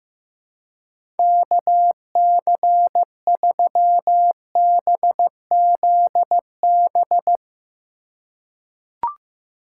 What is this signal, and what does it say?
Morse code, heard as a keyed tone, about 16 wpm, 700 Hz. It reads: KC3BZB
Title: USA Call Signs - 6 Characters - 4x Character Spacing - 15wpm